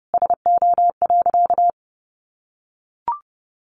Morse code, heard as a keyed tone, about 28 wpm, 700 Hz. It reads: SO.
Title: Sentences from Top 1400 Words 30wpm